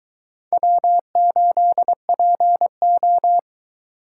Morse code, heard as a keyed tone, about 22 wpm, 700 Hz. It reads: W8PO